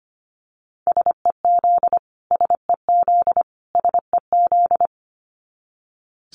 Morse code, heard as a keyed tone, about 25 wpm, 700 Hz. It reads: SEZ SEZ SEZ
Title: Abbreviations Repeated 3x - 25wpm